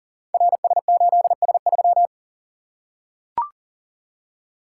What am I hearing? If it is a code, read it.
Morse code, heard as a keyed tone, about 42 wpm, 700 Hz. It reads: RS8S3
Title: Code Groups - 5 Letter:Numbers - 40wpm